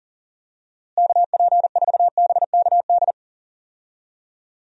Morse code, heard as a keyed tone, about 40 wpm, 700 Hz. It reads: KP4BKD
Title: DXCC-100 Call Signs - 6 Characters - Repeated 3x - 40wpm